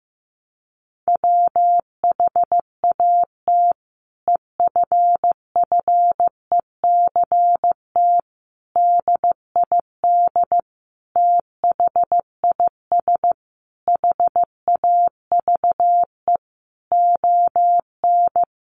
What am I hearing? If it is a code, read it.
Morse code, heard as a keyed tone, about 15 wpm, 700 Hz. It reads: WHAT EFFECT DID THIS HAVE ON
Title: Sentences from Top 1000 Words 15wpm